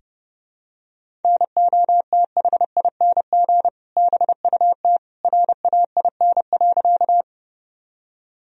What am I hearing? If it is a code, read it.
Morse code, heard as a keyed tone, about 30 wpm, 700 Hz. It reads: NOTHING BUT RAIN.
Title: Sentences from Top 1000 Words - Review Part 1 - 30wpm